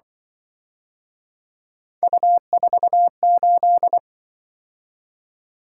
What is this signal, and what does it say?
Morse code, heard as a keyed tone, about 24 wpm, 700 Hz. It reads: U48
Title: Speed-Racing - Three Letter-Number Combinations - 20wpm